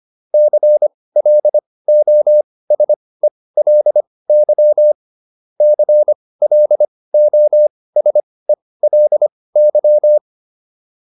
Morse code, heard as a keyed tone, about 25 wpm, spaced as for 18 wpm, 600 Hz. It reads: CLOSELY CLOSELY